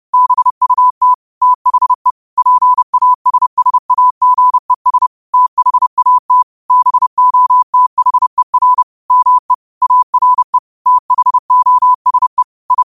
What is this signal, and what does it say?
Morse code, heard as a keyed tone, about 30 wpm, 1000 Hz. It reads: BUT THE PASSAGES THAT BOTHER ME ARE THOSE I
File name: qotdfiles / 30wpm